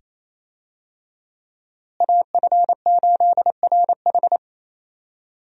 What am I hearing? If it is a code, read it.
Morse code, heard as a keyed tone, about 28 wpm, 700 Hz. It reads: AF8RH
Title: USA Call Signs - 5 Characters - Repeated 2x - 28wpm